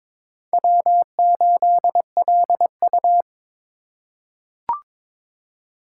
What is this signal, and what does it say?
Morse code, heard as a keyed tone, about 22 wpm, 700 Hz. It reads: W8LU
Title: USA Call Signs - 4 Characters - 2x Character Spacing - 22wpm